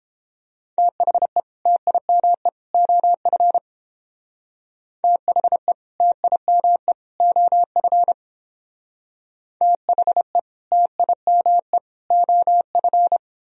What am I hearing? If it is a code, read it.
Morse code, heard as a keyed tone, about 29 wpm, 700 Hz. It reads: THE TIME OF THE TIME OF THE TIME OF